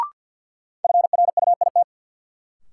Morse code, heard as a keyed tone, about 53 wpm, 700 Hz. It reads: FRUIT
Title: Sentences from Top 1300 Words - New Word Drill Alt - 50wpm